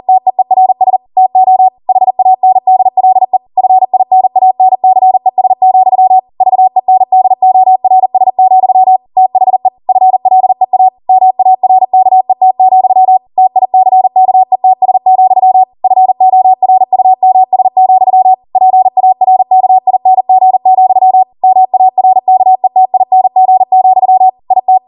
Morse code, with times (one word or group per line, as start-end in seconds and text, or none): 0.1-1.0 TEERS
1.2-1.7 TO
1.9-3.4 HANDLE
3.6-6.2 FINANCES,
6.4-9.0 VENDORS,
9.2-9.7 THE
9.9-10.9 FLEA
11.1-13.2 MARKET,
13.4-15.6 TICKETS,
15.8-18.3 FORUMS,
18.6-21.2 PARKING,
21.4-24.3 MARKETING,
24.5-24.8 IT